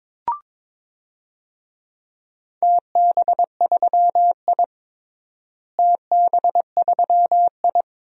Morse code, heard as a keyed tone, about 22 wpm, 700 Hz. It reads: TB3I TB3I
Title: DXCC-100 Call Signs - 4 Characters - Repeated 3x - 22wpm